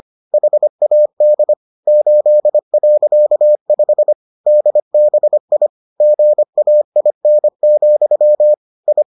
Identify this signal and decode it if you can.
Morse code, heard as a keyed tone, about 25 wpm, 600 Hz. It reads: HAD 8.5 DBI GAIN, I